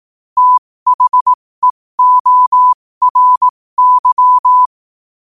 Morse code, heard as a keyed tone, about 18 wpm, spaced as for 15 wpm, 1000 Hz. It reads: THEORY
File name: qotdfiles / 15wpm